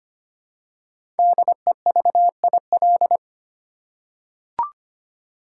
Morse code, heard as a keyed tone, about 25 wpm, 700 Hz. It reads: DEVIL
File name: 5 Letter Words - First 3 Letters Given - 25wpm